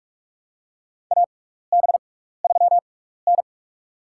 Morse code, heard as a keyed tone, about 46 wpm, spaced as for 15 wpm, 700 Hz. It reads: AB3N
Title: USA Call Signs - 4 Characters - 6x Character Spacing - 45wpm